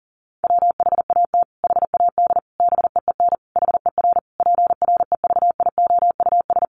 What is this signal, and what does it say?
Morse code, heard as a keyed tone, about 40 wpm, 700 Hz. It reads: WHAT HAD BEEN HER PREVIOUS
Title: Sentences from Top 1500 Words 40wpm